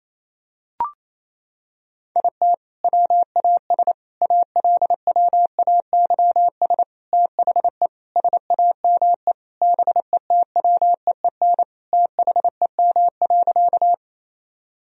Morse code, heard as a keyed tone, about 28 wpm, 700 Hz. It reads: IT WAS ALWAYS THE SAME BETWEEN THEM.
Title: Sentences from Top 1000 Words - Review Part 1 - 28wpm